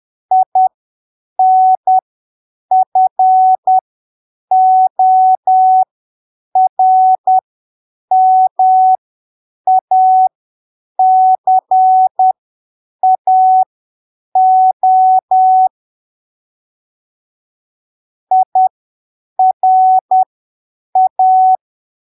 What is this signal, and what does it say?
Morse code, heard as a keyed tone, about 10 wpm, spaced as for 7 wpm, 750 Hz. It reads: INFORMACAO IRA
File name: arquivos / texto / texto 30 10ppm